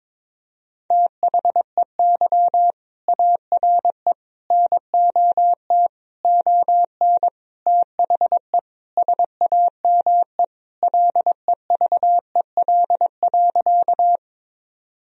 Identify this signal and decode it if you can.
Morse code, heard as a keyed tone, about 21 wpm, 700 Hz. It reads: THEY ARE NOT ON THE SAME LEVEL.
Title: Sentences from Top 1000 Words - Review Part 4 - 22wpm